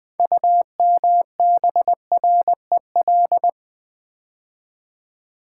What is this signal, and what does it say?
Morse code, heard as a keyed tone, about 20 wpm, 700 Hz. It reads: UMBREL
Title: Build A Word - 3 Letters At A Time - 8-Letter Words - 20wpm